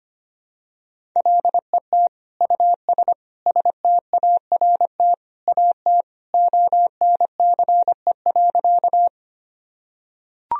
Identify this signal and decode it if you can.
Morse code, heard as a keyed tone, about 25 wpm, 700 Hz. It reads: LET US START AT ONCE.
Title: Sentences from Top 1000 Words - Review Part 1 - 25wpm